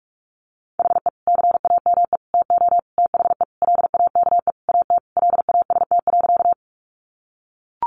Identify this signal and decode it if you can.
Morse code, heard as a keyed tone, about 44 wpm, 700 Hz. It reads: HE CAME TO THE LAKE AT LAST.
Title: Sentences from Top 1100 Words 45wpm